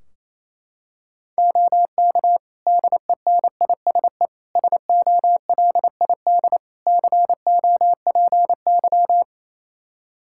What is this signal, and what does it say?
Morse code, heard as a keyed tone, about 28 wpm, 700 Hz. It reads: OK DENISE SOLID COPY
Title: K1USN SST QSO Practice 28wpm